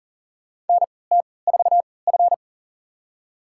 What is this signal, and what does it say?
Morse code, heard as a keyed tone, about 40 wpm, spaced as for 23 wpm, 700 Hz. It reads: NT4F
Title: USA Call Signs - 4 Characters - 3x Character Spacing - 40wpm